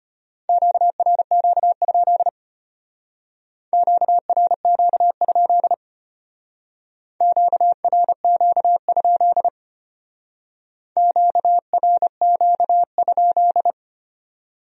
Morse code, heard as a keyed tone, about 32 wpm, 700 Hz. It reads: QRQ? QRQ? QRQ? QRQ?